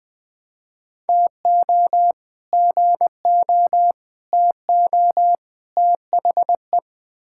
Morse code, heard as a keyed tone, about 20 wpm, 700 Hz. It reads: TO GO TO THE
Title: Build A Phrase - 4-Word Phrases - 20wpm